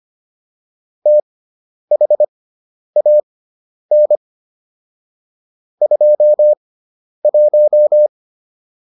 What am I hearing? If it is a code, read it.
Morse code, heard as a keyed tone, about 25 wpm, spaced as for 10 wpm, 600 Hz. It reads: THAN 21